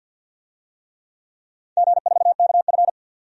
Morse code, heard as a keyed tone, about 51 wpm, 700 Hz. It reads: G4KF
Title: DXCC-100 Call Signs - 4 Characters - 50wpm